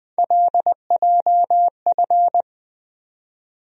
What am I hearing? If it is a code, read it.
Morse code, heard as a keyed tone, about 20 wpm, 700 Hz. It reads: LJF
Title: USA Call Sign Postfixes - 3 Letters 20wpm